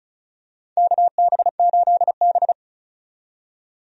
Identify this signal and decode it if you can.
Morse code, heard as a keyed tone, about 36 wpm, 700 Hz. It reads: KB8B
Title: USA Call Signs - 4 Characters - 35wpm